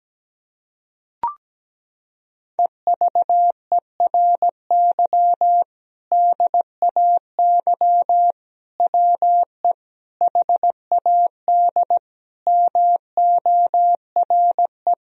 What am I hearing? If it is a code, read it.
Morse code, heard as a keyed tone, about 17 wpm, 700 Hz. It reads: EVERY DAY WE HAD MORE
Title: Sentences from Top 1000 Words - Review Part 3 - 17wpm